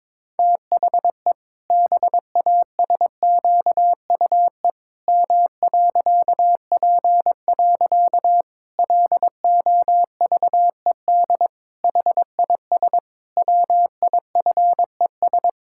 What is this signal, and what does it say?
Morse code, heard as a keyed tone, about 22 wpm, 700 Hz. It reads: THE BASQUE M.P. LOVED HIS WIFES
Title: Pangram Challenge v2 - 22wpm